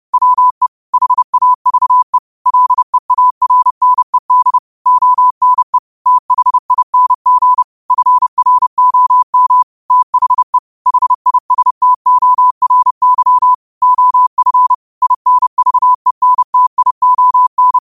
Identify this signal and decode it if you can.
Morse code, heard as a keyed tone, about 30 wpm, 1000 Hz. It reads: WE HAVE LEARNED ONE THING FROM THE HISTORY OF INVENTION
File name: 30wpm